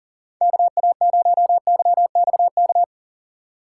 Morse code, heard as a keyed tone, about 40 wpm, 700 Hz. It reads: KA0YXK
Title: USA Call Signs - 3x Character Spacing - 40wpm